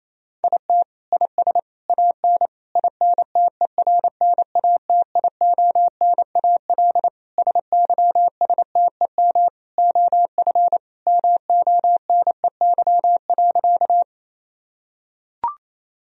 Morse code, heard as a keyed tone, about 27 wpm, 700 Hz. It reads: IT IS AN INTERNATIONAL SYSTEM OF MONEY.
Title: Sentences from Top 1000 Words - Review Part 4 - 28wpm